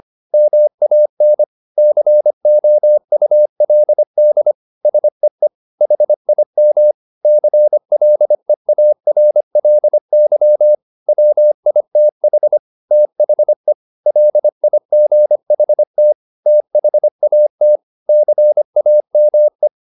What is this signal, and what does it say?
Morse code, heard as a keyed tone, about 25 wpm, 600 Hz. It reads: MAN COULD SEE HIM CLEARLY WITH THE LIGHT THAT CAME
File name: the old man and the sea part 1 25wpm